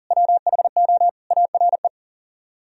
Morse code, heard as a keyed tone, about 49 wpm, 700 Hz. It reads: WHO ARE
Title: Speed-Racing - Sets of 3 Words 40wpm